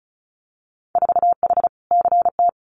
Morse code, heard as a keyed tone, about 36 wpm, 700 Hz. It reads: 4H CT
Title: Winter Field Day Exchanges Encore 35wpm